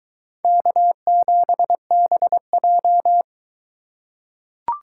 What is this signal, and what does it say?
Morse code, heard as a keyed tone, about 23 wpm, 700 Hz. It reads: K7BJ